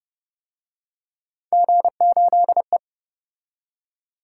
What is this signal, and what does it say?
Morse code, heard as a keyed tone, about 28 wpm, 700 Hz. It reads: G8E